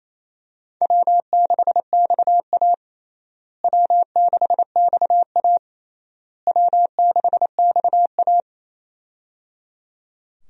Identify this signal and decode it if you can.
Morse code, heard as a keyed tone, about 28 wpm, 700 Hz. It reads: W6XA W6XA W6XA
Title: USA Call Signs - 4 Characters - Repeated 3x - 28wpm